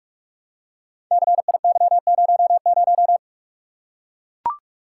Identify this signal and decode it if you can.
Morse code, heard as a keyed tone, about 44 wpm, 700 Hz. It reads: CIY00
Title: Code Groups - 5 Letter:Numbers - 45wpm